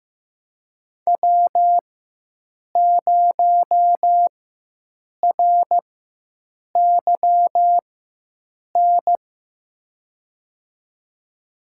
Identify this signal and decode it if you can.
Morse code, heard as a keyed tone, about 15 wpm, spaced as for 7 wpm, 700 Hz. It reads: W0RYN
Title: USA Call Signs - 5 Characters - 4x Character Spacing - 15wpm